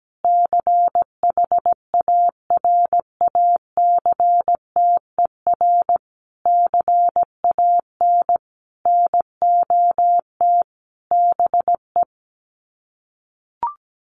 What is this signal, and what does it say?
Morse code, heard as a keyed tone, about 17 wpm, 700 Hz. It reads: CHARACTER CAN NOT BE
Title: Sets of 5 Words Encore2 17wpm